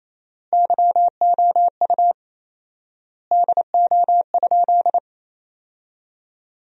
Morse code, heard as a keyed tone, about 28 wpm, 700 Hz. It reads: YOU DO?
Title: 4x Word-Spacing - Sentences from Top 100 Words - 28wpm